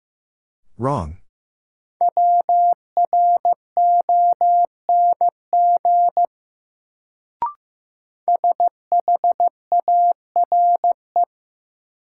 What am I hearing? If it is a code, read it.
Morse code, heard as a keyed tone, about 15 wpm, 700 Hz. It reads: WRONG SHARE
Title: Sentences from Top 900 Words - New Word Drill - 15wpm